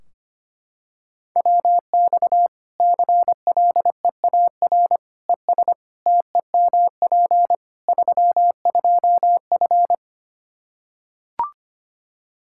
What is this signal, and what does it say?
Morse code, heard as a keyed tone, about 25 wpm, 700 Hz. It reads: WX CLEAR ES TEMP 32F
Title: Practice QSO - Exchange 2 - Info - 25wpm